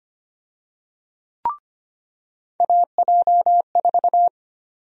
Morse code, heard as a keyed tone, about 25 wpm, 700 Hz. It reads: AJ4
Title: USA Call Sign Prefixes - 25wpm